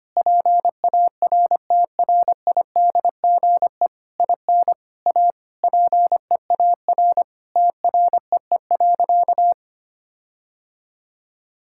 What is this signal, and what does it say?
Morse code, heard as a keyed tone, about 25 wpm, 700 Hz. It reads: PARTRIDGE IN A PEAR TREE.
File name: practice / 25wpm / Sentences with 9-Word Phrases - 25wpm